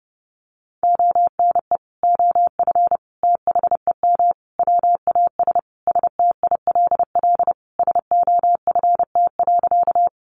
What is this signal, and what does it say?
Morse code, heard as a keyed tone, about 28 wpm, 700 Hz. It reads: ONE OF THEM WAS STILL SOFT.
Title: Sentences from Top 1500 Words 30wpm